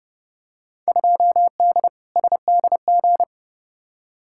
Morse code, heard as a keyed tone, about 31 wpm, 700 Hz. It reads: 2D SDG
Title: Field Day Exchanges Encore - 30wpm